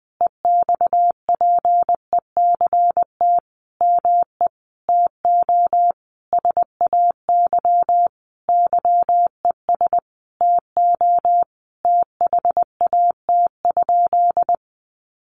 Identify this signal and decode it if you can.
Morse code, heard as a keyed tone, about 20 wpm, 700 Hz. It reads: EXPECT ME TO SAY YES TO THAT?
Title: Sentences from Top 1100 Words 20wpm